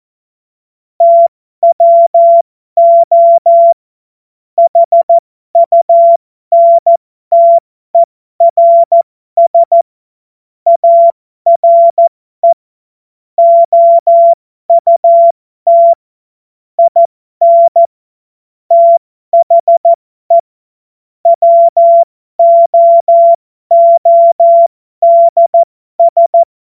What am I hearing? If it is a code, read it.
Morse code, heard as a keyed tone, about 14 wpm, spaced as for 12 wpm, 680 Hz. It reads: TWO HUNTERS ARE OUT IN THE WOODS
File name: Morsekurs / Witze